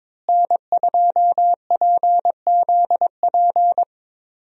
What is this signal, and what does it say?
Morse code, heard as a keyed tone, about 22 wpm, 700 Hz. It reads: N2PZP
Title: USA Call Signs - 5 Characters - Repeated 2x - 22wpm